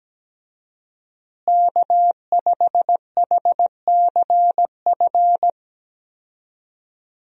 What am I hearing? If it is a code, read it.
Morse code, heard as a keyed tone, about 17 wpm, 700 Hz. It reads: K5HCF
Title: USA Call Signs - 5 Characters - Repeated 3x - 17wpm